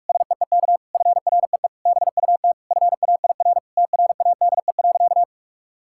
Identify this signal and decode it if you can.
Morse code, heard as a keyed tone, about 44 wpm, 700 Hz. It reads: SEEK FREE BUT FAIR TRADE.
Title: Sentences from Top 1600 Words 45wpm